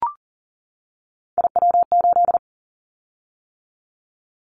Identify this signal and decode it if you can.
Morse code, heard as a keyed tone, about 39 wpm, 700 Hz. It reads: IW8
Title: Callsign Prefixes for Top 25 DXCC - 40wpm